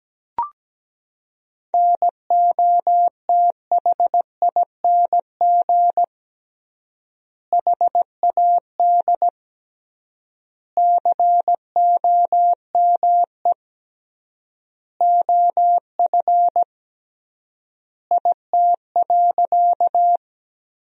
Morse code, heard as a keyed tone, about 17 wpm, 700 Hz. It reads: NOTHING HAD COME OF IT.
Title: 3x Word-Spacing - Sentences from Top 500 Words - 17wpm